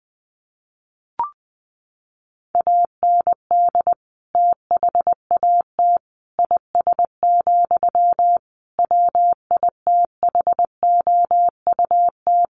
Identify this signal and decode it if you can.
Morse code, heard as a keyed tone, about 19 wpm, 700 Hz. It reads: AND THAT IS, WITHOUT
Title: Sentences from Top 1500 Words 20wpm